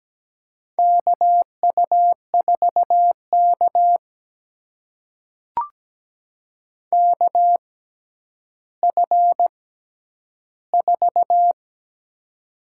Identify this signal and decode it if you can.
Morse code, heard as a keyed tone, about 17 wpm, spaced as for 10 wpm, 700 Hz. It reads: KU4K KF4
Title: USA Call Signs - 6x Character Spacing - 17wpm